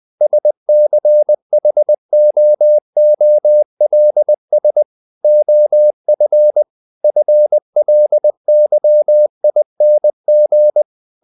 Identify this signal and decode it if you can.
Morse code, heard as a keyed tone, about 20 wpm, 600 Hz. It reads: SCHOOLS OF FLYING
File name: the old man and the sea part 2 20wpm